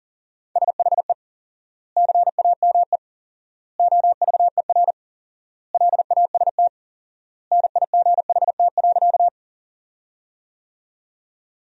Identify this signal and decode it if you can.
Morse code, heard as a keyed tone, about 40 wpm, 700 Hz. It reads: SHE CAME OVER LAST NIGHT.
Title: 4x Word-Spacing - Sentences from Top 300 Words - 40wpm